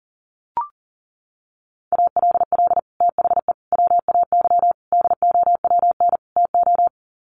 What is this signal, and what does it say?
Morse code, heard as a keyed tone, about 40 wpm, 700 Hz. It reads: ALL THE WAY DOWN TO
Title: Sets of 5 Words Encore2 40wpm